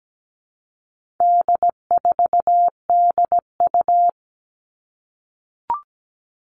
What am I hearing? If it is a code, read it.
Morse code, heard as a keyed tone, about 17 wpm, 700 Hz. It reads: D4DU
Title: All Cut Numbers - 4-Digit 17wpm